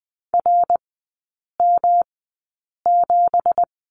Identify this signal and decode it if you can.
Morse code, heard as a keyed tone, about 20 wpm, spaced as for 8 wpm, 700 Hz. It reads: RM7
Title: 166 Lesson 27e - Number 7 - Three Characters 20wpm